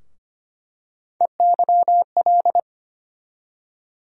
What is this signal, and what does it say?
Morse code, heard as a keyed tone, about 25 wpm, 700 Hz. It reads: EYL
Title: USA Call Sign Postfixes - 3 Letters 25wpm